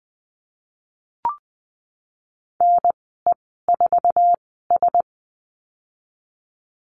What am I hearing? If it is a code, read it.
Morse code, heard as a keyed tone, about 20 wpm, spaced as for 15 wpm, 700 Hz. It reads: NE4S
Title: 088 Lesson 15g - Number 5 - Call Signs 20wpm